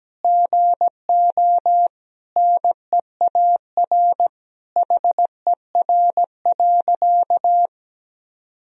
Morse code, heard as a keyed tone, about 17 wpm, 700 Hz. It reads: GO NEAR HER.
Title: Sentences from Top 1000 Words - Review Part 4 - 17wpm